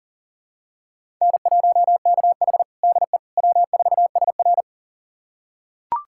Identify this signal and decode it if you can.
Morse code, heard as a keyed tone, about 37 wpm, 700 Hz. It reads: N1KH DE W4SR
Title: USA Callsign DE Callsign - 4 Characters - 40wpm